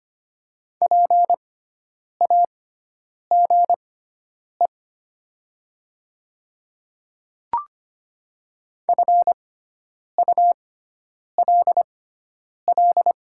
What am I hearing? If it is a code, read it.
Morse code, heard as a keyed tone, about 25 wpm, spaced as for 9 wpm, 700 Hz. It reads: PAGE FULL
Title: 6x Character Spacing - Top 500 Words - 25wpm